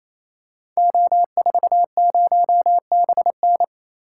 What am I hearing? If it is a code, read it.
Morse code, heard as a keyed tone, about 28 wpm, 700 Hz. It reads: O40BN
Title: Code Groups - 5 Letter:Numbers - 28wpm